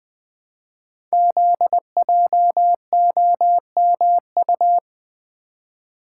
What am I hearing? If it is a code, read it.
Morse code, heard as a keyed tone, about 20 wpm, 700 Hz. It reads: ZJOMU
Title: Code Groups - 5 Letters - 20wpm